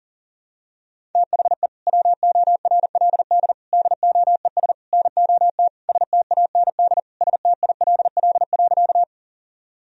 Morse code, heard as a keyed tone, about 37 wpm, 700 Hz. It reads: THE WORLD DOES NOT STAND STILL.
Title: Sentences from Top 1000 Words - Review Part 1 - 40wpm